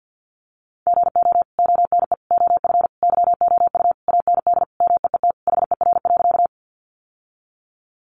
Morse code, heard as a keyed tone, about 51 wpm, 700 Hz. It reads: GO ONE OF YOU AND MEET HER.
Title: Sentences from Top 1100 Words 50wpm